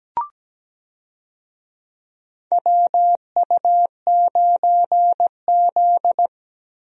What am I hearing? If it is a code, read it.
Morse code, heard as a keyed tone, about 17 wpm, 700 Hz. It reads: WU9Z
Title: USA Call Signs - 4 Characters - Repeated 3x - 17wpm